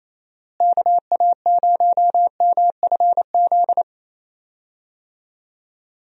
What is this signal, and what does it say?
Morse code, heard as a keyed tone, about 28 wpm, 700 Hz. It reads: KA0MFZ
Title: USA Call Signs - 6 Characters - Repeated 3x - 28wpm